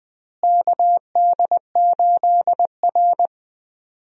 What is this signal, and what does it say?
Morse code, heard as a keyed tone, about 20 wpm, 700 Hz. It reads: KD8R
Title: USA Call Signs - 4 Characters - Repeated 3x - 20wpm